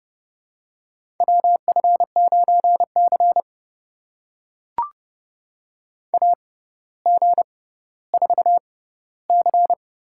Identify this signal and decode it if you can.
Morse code, heard as a keyed tone, about 32 wpm, spaced as for 16 wpm, 700 Hz. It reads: WF9C AG4C
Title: USA Call Signs - 4 Characters - 6x Character Spacing - 30wpm